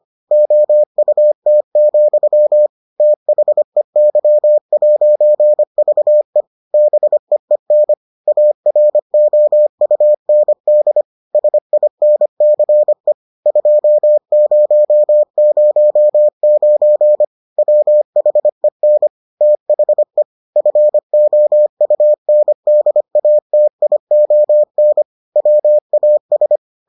Morse code, heard as a keyed tone, about 25 wpm, 600 Hz. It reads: OUT, THEY'VE BEEN AROUND SINCE 2009 WHEN THE FOUNDATION WAS